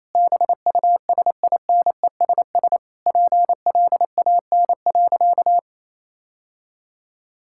Morse code, heard as a keyed tone, about 28 wpm, 700 Hz. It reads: BUSINESS PLAN.
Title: Sentences from Top 1000 Words - Review Part 6 - 28wpm